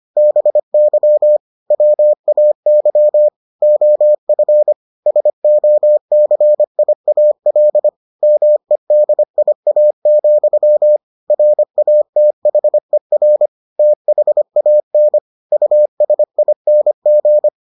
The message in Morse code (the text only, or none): BY WAY OF SOCIAL MEDIA, RATHER THAN USING